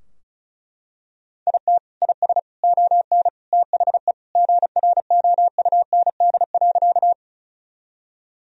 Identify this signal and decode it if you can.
Morse code, heard as a keyed tone, about 34 wpm, 700 Hz. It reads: IT IS ON THE GROUND.
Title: Sentences from Top 1000 Words - Review Part 2 - 35wpm